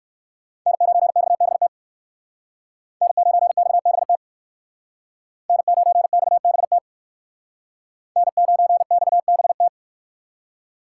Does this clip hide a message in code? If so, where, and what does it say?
Yes: Morse code from 0.7 to 9.7 s, heard as a keyed tone, about 58 wpm, 700 Hz. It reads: N9XBT N9XBT N9XBT N9XBT